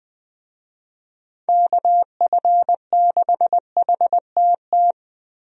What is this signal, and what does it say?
Morse code, heard as a keyed tone, about 21 wpm, 700 Hz. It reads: KF6HTT
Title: USA Call Signs - 6 Characters - Repeated 3x - 20wpm